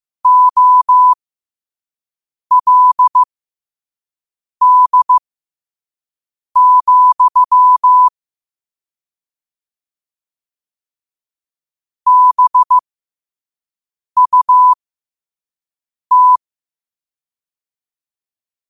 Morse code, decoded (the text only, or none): OLD, BUT